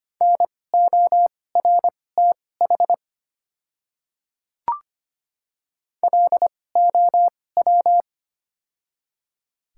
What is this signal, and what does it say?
Morse code, heard as a keyed tone, about 25 wpm, spaced as for 18 wpm, 700 Hz. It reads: NORTH LOW
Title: 2x Character Spacing - Top 300 Words - 25wpm